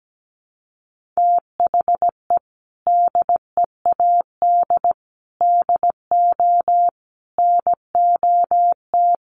Morse code, heard as a keyed tone, about 17 wpm, 700 Hz. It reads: THE DEAD DO NOT
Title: Sentences from Top 1500 Words 17wpm